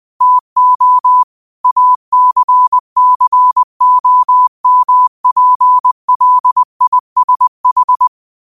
Morse code, heard as a keyed tone, about 20 wpm, 1000 Hz. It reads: TO ACCOMPLISH
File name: 20wpm